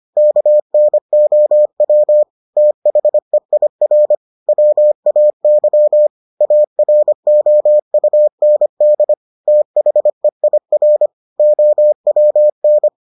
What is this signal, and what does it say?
Morse code, heard as a keyed tone, about 25 wpm, 600 Hz. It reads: KNOW THEIR WAY AROUND THEIR OWN